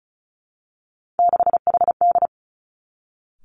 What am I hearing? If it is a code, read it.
Morse code, heard as a keyed tone, about 36 wpm, 700 Hz. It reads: 6HD
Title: Three Challenging Letter-Number Combinations 35wpm